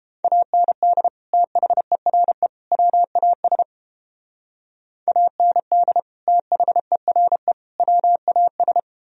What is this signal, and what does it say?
Morse code, heard as a keyed tone, about 29 wpm, 700 Hz. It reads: AND THERE WAS AND THERE WAS